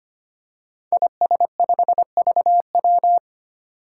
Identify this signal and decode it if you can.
Morse code, heard as a keyed tone, about 25 wpm, 700 Hz. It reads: IS5VW
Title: Five Challenging Letter-Number Combinations 25wpm